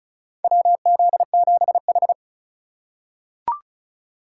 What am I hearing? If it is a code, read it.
Morse code, heard as a keyed tone, about 34 wpm, 700 Hz. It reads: WZ7H
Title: USA Call Signs - 4 Characters - 4x Character Spacing - 35wpm